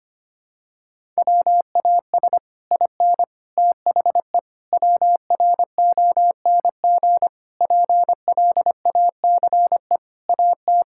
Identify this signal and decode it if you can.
Morse code, heard as a keyed tone, about 25 wpm, 700 Hz. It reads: WAS IN THE WRONG PLACE AT
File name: Sets of 9 Words - 25wpm